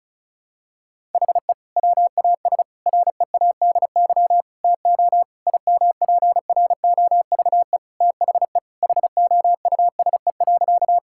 Morse code, heard as a keyed tone, about 36 wpm, 700 Hz. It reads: HE WAS READY TO IMPROVE THE HOUSE.
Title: Sentences from Top 1000 Words - Review Part 5 - 35wpm